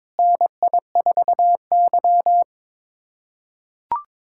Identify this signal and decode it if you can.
Morse code, heard as a keyed tone, about 22 wpm, 700 Hz. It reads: NI4Y
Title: USA Callsign DE Callsign - 4 Characters - 22wpm